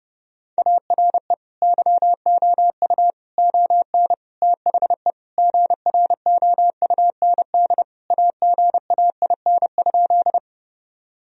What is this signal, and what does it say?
Morse code, heard as a keyed tone, about 32 wpm, 700 Hz. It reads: ARE YOU ON THE GROUND AGAIN?
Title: Sentences from Top 1000 Words - Review Part 2 - 30wpm